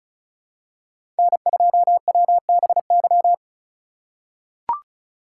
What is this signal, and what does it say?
Morse code, heard as a keyed tone, about 34 wpm, 700 Hz. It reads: N2WBY
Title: USA Call Signs - 5 Characters - 3x Character Spacing - 35wpm